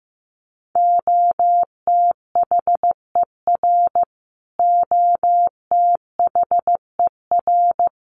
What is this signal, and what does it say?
Morse code, heard as a keyed tone, about 15 wpm, 700 Hz. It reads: OTHER OTHER
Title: Top 500 Words Repeated 15wpm